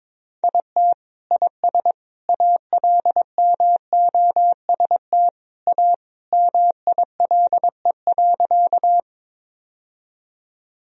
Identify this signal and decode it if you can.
Morse code, heard as a keyed tone, about 21 wpm, 700 Hz. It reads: IT IS ALMOST A MILE.
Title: Sentences from Top 1000 Words - Review Part 4 - 22wpm